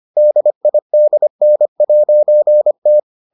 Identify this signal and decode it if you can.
Morse code, heard as a keyed tone, about 25 wpm, 600 Hz. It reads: DIDN'T